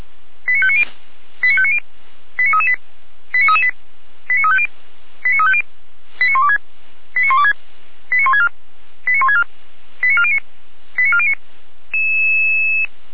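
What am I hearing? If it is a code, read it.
Five-tone selective call (ZVEI), heard as quick runs of five tones, tones of about 70 ms. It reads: 89590 89590 89338 89338 89350 89350 89126 89126 89164 89164 89409 89409
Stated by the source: Received on an SDR, ZVEI five-tone call 89701.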